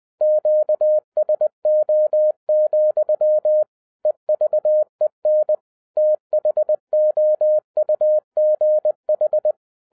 Morse code, heard as a keyed tone, about 20 wpm, 610 Hz. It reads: QSO, EVEN THOUGH